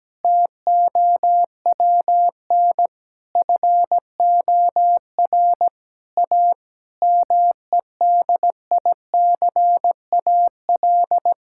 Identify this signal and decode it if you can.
Morse code, heard as a keyed tone, about 17 wpm, 700 Hz. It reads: TOWN FOR A MEDICAL
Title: Sentences from Top 1000 Words - Review Part 5 - 17wpm